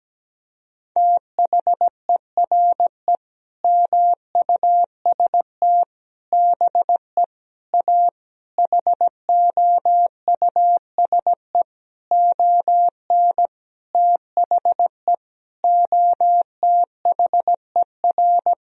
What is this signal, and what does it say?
Morse code, heard as a keyed tone, about 17 wpm, 700 Hz. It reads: THERE MUST BE A HOUSE ON THE OTHER
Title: Sentences from Top 1000 Words - Review Part 1 - 17wpm